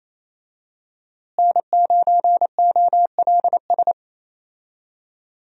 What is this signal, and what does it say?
Morse code, heard as a keyed tone, about 27 wpm, 700 Hz. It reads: N9OLS
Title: USA Call Signs - 5 Characters - Repeated 2x - 28wpm